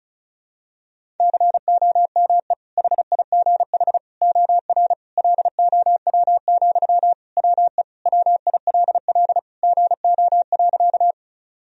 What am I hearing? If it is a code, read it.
Morse code, heard as a keyed tone, about 36 wpm, 700 Hz. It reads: COME HIGH OR LOW, WE WILL GO.
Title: Sentences from Top 1000 Words - Review Part 1 - 35wpm